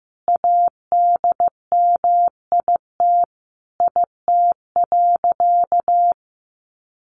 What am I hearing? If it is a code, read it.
Morse code, heard as a keyed tone, about 15 wpm, 700 Hz. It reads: ADMIT IT.
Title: A Sound Of Thunder 15wpm